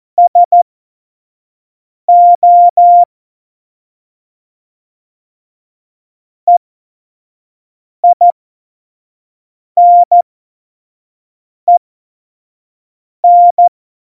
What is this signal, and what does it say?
Morse code, heard as a keyed tone, about 14 wpm, spaced as for 5 wpm, 700 Hz. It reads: SO EINEN
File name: Morsekurs / Witze / Witz 11 August 2020, 25 F70